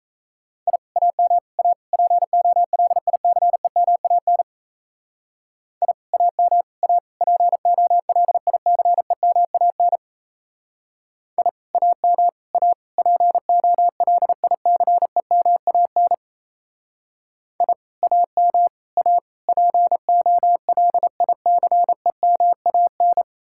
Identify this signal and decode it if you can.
Morse code, heard as a keyed tone, about 33 wpm, 700 Hz. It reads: I AM A POLICEMAN I AM A POLICEMAN I AM A POLICEMAN I AM A POLICEMAN
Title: Speed-Racing Common QSO Phrases - 28wpm